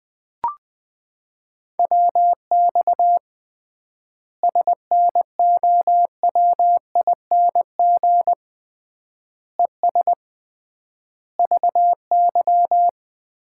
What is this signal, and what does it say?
Morse code, heard as a keyed tone, about 21 wpm, 700 Hz. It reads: WX SNOWING ES VY
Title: Common QSO Phrases - 3x Word-Spacing - 20wpm